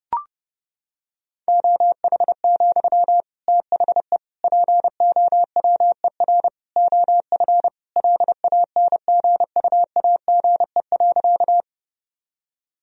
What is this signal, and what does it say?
Morse code, heard as a keyed tone, about 32 wpm, 700 Hz. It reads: OH, THE POWER OF LANGUAGE.
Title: Sentences from Top 1000 Words - Review Part 3 - 30wpm